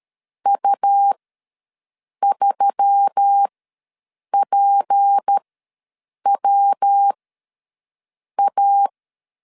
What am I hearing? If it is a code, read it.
Morse code, heard as a keyed tone, about 13 wpm, spaced as for 6 wpm, 800 Hz. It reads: U3PWA
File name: LEZIONI / 6 PKX0 / 6 P ws 12wpm 3117